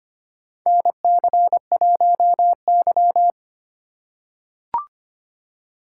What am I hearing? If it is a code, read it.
Morse code, heard as a keyed tone, about 25 wpm, 700 Hz. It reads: NC1Y
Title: USA Call Signs - 4 Characters - 3x Character Spacing - 25wpm